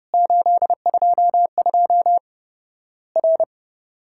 Morse code, heard as a keyed tone, about 30 wpm, 700 Hz. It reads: 822 R
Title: Worked All Europe DX Contest - QTC Exchange - 30wpm